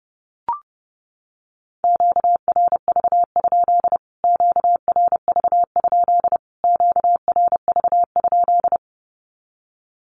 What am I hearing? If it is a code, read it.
Morse code, heard as a keyed tone, about 30 wpm, 700 Hz. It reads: QRV? QRV? QRV?
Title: Q-Codes Repeated 3x 30wpm